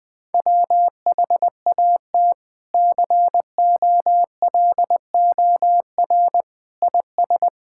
Morse code, heard as a keyed tone, about 20 wpm, 700 Hz. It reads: WHAT COLOR IS
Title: Sentences from Top 1000 Words - Review Part 6 - 20wpm